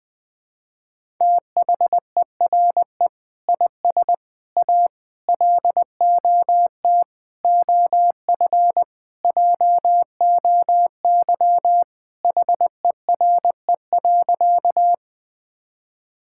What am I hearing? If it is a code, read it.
Morse code, heard as a keyed tone, about 20 wpm, 700 Hz. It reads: THERE IS A LOT OF JOY HERE.
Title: Sentences with 5-Word Phrases - 20wpm